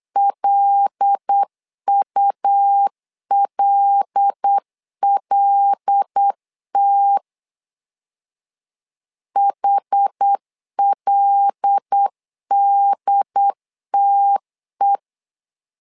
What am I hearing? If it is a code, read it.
Morse code, heard as a keyed tone, about 8 wpm, 800 Hz. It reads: LULLT HLDTE
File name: LEZIONI / 5 LBVC / 5 L 8wpm 2831